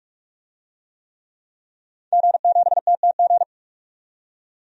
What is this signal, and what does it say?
Morse code, heard as a keyed tone, about 44 wpm, 700 Hz. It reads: G7TTG
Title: DXCC-100 Call Signs - 5 Characters - 45wpm